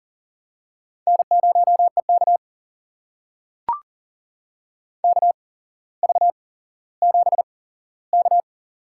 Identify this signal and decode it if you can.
Morse code, heard as a keyed tone, about 40 wpm, spaced as for 17 wpm, 700 Hz. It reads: N0EK KV7K
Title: USA Call Signs - 4 Characters - 8x Character Spacing - 40wpm